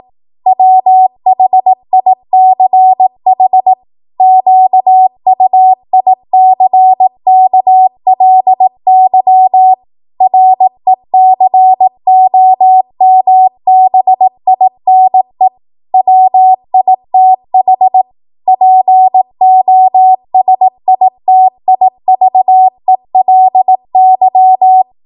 Morse code, heard as a keyed tone, about 18 wpm, 750 Hz. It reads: WHICH QUICKLY RECOMBINE WITH POSITIVELY
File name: Morse / 18WPM